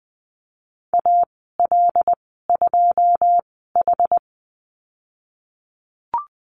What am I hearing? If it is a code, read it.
Morse code, heard as a keyed tone, about 20 wpm, spaced as for 14 wpm, 700 Hz. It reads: AL2H
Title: 193 Lesson 31g - Letter v - Call Signs 20wpm